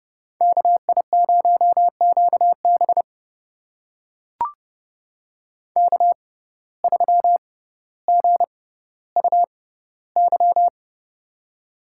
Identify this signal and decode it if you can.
Morse code, heard as a keyed tone, about 30 wpm, spaced as for 15 wpm, 700 Hz. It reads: KI0QB K3GUY
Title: USA Call Signs - 6x Character Spacing - 30wpm